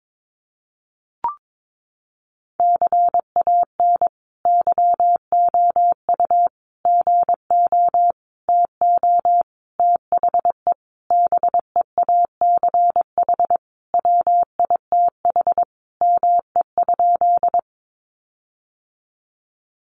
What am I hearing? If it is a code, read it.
Morse code, heard as a keyed tone, about 22 wpm, 700 Hz. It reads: CAN YOU GO TO THE BEACH WITH ME?
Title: Sentences from Top 1500 Words 22wpm